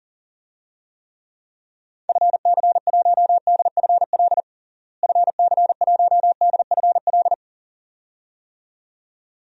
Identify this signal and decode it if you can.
Morse code, heard as a keyed tone, about 40 wpm, 700 Hz. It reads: FC1DFL FC1DFL
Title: DXCC-100 Call Signs - 6 Characters - Repeated 2x - 40wpm